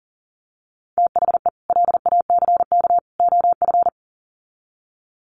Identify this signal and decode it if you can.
Morse code, heard as a keyed tone, about 40 wpm, 700 Hz. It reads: THE LACK OF
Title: Sets of 3 Words 40wpm